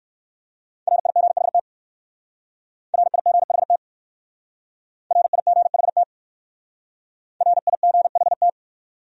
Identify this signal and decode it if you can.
Morse code, heard as a keyed tone, about 56 wpm, 700 Hz. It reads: RIGHT RIGHT RIGHT RIGHT
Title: Speed-Racing Top 300 Words 45wpm